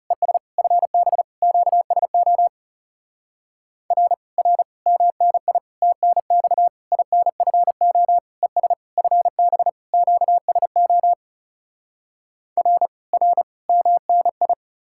Morse code, heard as a keyed tone, about 40 wpm, 700 Hz. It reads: ES FB QSO R R MNI TNX INFO ES FB QSO R R MNI